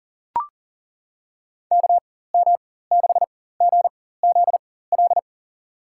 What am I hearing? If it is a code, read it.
Morse code, heard as a keyed tone, about 40 wpm, spaced as for 19 wpm, 700 Hz. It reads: KM6GZL
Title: USA Call Signs - 6 Characters - 4x Character Spacing - 40wpm